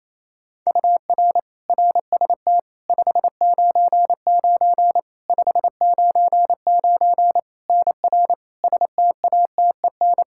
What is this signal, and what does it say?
Morse code, heard as a keyed tone, about 27 wpm, 700 Hz. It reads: UR RST 599 599 NR STATEN
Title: Practice QSO - Exchange 1 - Beginning - 28wpm